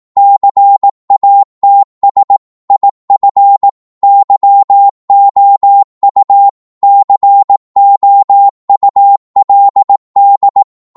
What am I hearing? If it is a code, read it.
Morse code, heard as a keyed tone, about 18 wpm, 800 Hz. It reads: CATS IF YOU COULD
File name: Morse / psg-18wpm